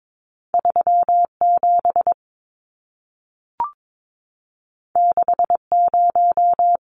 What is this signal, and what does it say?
Morse code, heard as a keyed tone, about 22 wpm, 700 Hz. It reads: 37 60
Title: Numbers - 2-Digit 22wpm